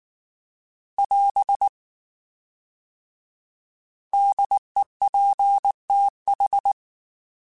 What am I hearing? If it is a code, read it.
Morse code, heard as a keyed tone, about 19 wpm, 800 Hz. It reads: <AS> DEPTH